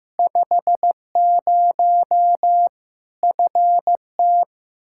Morse code, heard as a keyed tone, about 15 wpm, 700 Hz. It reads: 50 FT
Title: Practice QSO - Exchange 2 - Info - 15wpm